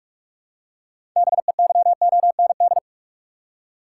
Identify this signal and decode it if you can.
Morse code, heard as a keyed tone, about 46 wpm, 700 Hz. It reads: BEYOND
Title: Six Letter Words - 45wpm